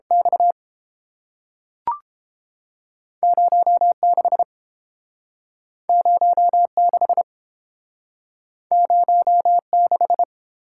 Morse code, heard as a keyed tone, about 31 wpm, 700 Hz. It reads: X 0 6 0 6 0 6